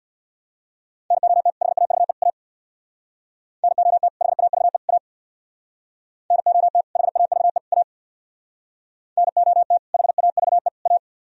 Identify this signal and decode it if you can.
Morse code, heard as a keyed tone, about 65 wpm, 700 Hz. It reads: NOT HAVE A NOT HAVE A NOT HAVE A NOT HAVE A